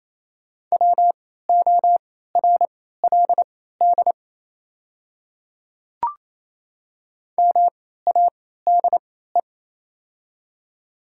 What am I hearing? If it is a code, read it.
Morse code, heard as a keyed tone, about 28 wpm, spaced as for 16 wpm, 700 Hz. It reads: WORLD MADE
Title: 3x Character Spacing - Top 200 Words - 28wpm